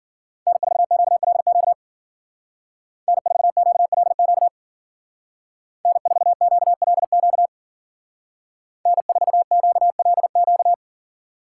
Morse code, heard as a keyed tone, about 53 wpm, 700 Hz. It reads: N4QLQ N4QLQ N4QLQ N4QLQ